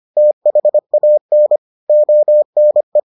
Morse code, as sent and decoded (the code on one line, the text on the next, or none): - .... .- -. / --- -. .
THAN ONE